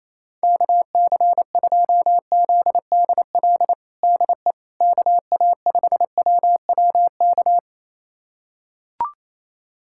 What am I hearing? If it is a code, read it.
Morse code, heard as a keyed tone, about 28 wpm, 700 Hz. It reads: KC2ZDL DE KA5WWK
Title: USA Callsign DE Callsign 28wpm